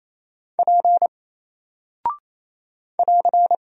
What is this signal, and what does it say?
Morse code, heard as a keyed tone, about 28 wpm, 700 Hz. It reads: P <AR>
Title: Single Character (Letter:Number:Punctuation:Prosign) 28wpm